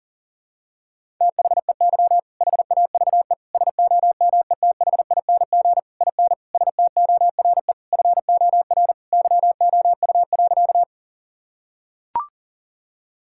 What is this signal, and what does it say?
Morse code, heard as a keyed tone, about 37 wpm, 700 Hz. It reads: THEY HAVE SOMETHING IN STORE FOR YOU.
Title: Sentences from Top 1000 Words - Review Part 5 - 40wpm